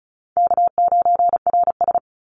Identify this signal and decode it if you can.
Morse code, heard as a keyed tone, about 34 wpm, 700 Hz. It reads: K9RS
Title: Call Signs Repeated 2x - 35wpm